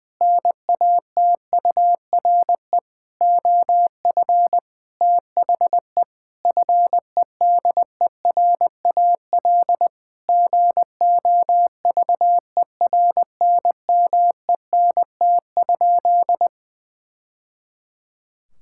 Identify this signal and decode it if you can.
Morse code, heard as a keyed tone, about 20 wpm, 700 Hz. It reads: NATURE OF THE FEDERAL GOVERNMENT?
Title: Sentences from Top 1000 Words - Review Part 5 - 20wpm